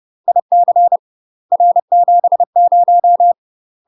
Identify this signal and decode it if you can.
Morse code, heard as a keyed tone, about 30 wpm, 700 Hz. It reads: IC R70
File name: UK MASTER QSOs TRAINING / 30WPM / UK MASTER QSOs 0047